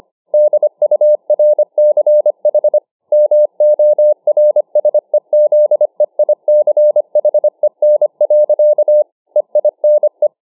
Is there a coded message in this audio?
Yes: Morse code from 0.3 to 10.3 s, heard as a keyed tone, about 25 wpm, 600 Hz. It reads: DURCH MORSEZEICHEN. EINE